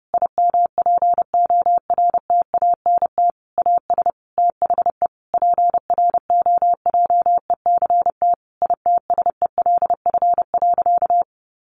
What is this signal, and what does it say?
Morse code, heard as a keyed tone, about 30 wpm, 700 Hz. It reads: IMPORTANT AS THE PROJECT ITSELF.
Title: Sentences from Top 1400 Words 30wpm